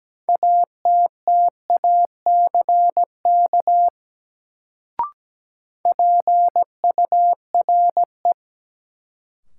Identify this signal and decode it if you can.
Morse code, heard as a keyed tone, about 17 wpm, 700 Hz. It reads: ATTACK PURE
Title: Sentences from Top 1500 Words - New Word Drill Alt - 17wpm